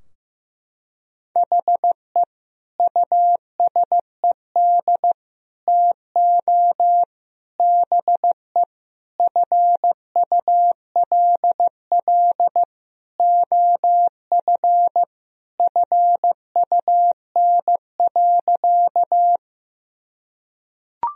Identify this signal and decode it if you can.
Morse code, heard as a keyed tone, about 15 wpm, 700 Hz. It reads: HE USED TO BE FULL OF FUN.
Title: Sentences from Top 1000 Words - Review Part 4 - 15wpm